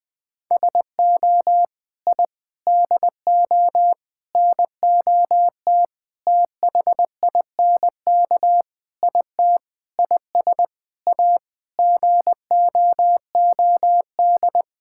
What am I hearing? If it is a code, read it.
Morse code, heard as a keyed tone, about 20 wpm, 700 Hz. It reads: SO I DO NOT THINK IT IS A GOOD
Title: Sentences with 5-Word Phrases - 20wpm